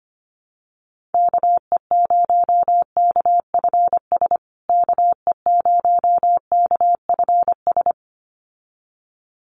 Morse code, heard as a keyed tone, about 25 wpm, 700 Hz. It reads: KE0KFS KE0KFS
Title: USA Call Signs Repeated 2x 25wpm